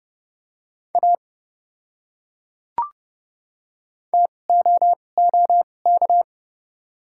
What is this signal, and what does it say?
Morse code, heard as a keyed tone, about 30 wpm, spaced as for 22 wpm, 700 Hz. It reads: A TOOK
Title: 2x Character Spacing - Top 300 Words - 30wpm